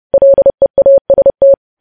Morse code, heard as a keyed tone, about 30 wpm, 550 Hz. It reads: LEAST
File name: podcast / News headlines-1x-30x30